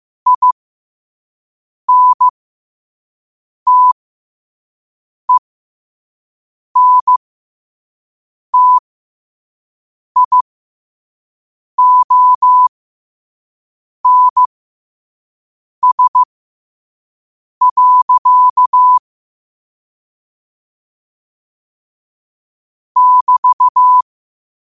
Morse code, heard as a keyed tone, about 15 wpm, spaced as for 5 wpm, 1000 Hz. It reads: INTENTIONS. <BT>
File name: qotdfiles / 5wpm